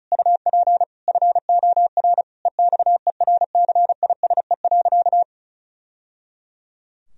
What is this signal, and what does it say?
Morse code, heard as a keyed tone, about 34 wpm, 700 Hz. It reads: UP FOR EXERCISE.
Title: Sentences from Top 1000 Words - Review Part 4 - 35wpm